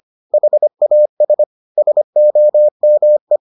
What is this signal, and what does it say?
Morse code, heard as a keyed tone, about 25 wpm, 600 Hz. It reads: HAS SOME